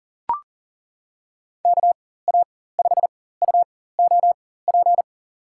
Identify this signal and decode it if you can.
Morse code, heard as a keyed tone, about 37 wpm, spaced as for 18 wpm, 700 Hz. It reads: KA5UOP
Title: USA Call Signs - 6 Characters - 4x Character Spacing - 40wpm